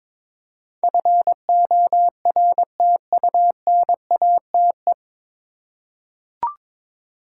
Morse code, heard as a keyed tone, about 22 wpm, 700 Hz. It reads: FORTUNATE
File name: 9 Letter Words - First 3 Letters Given - 22wpm